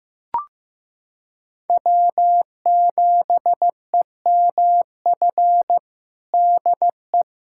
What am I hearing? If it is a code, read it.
Morse code, heard as a keyed tone, about 15 wpm, 700 Hz. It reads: W7EMF DE
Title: USA Callsign DE Callsign - 5 Characters - 15wpm